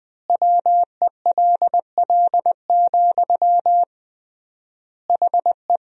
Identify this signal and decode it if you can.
Morse code, heard as a keyed tone, about 20 wpm, 700 Hz. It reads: WELL, HE
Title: 3x Word-Spacing - Sentences from Top 500 Words - 20wpm